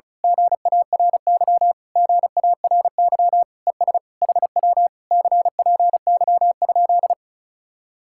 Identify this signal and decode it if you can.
Morse code, heard as a keyed tone, about 33 wpm, 700 Hz. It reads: GARY GARY ES HW CPY?
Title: Speed-Racing Common QSO Phrases - 35wpm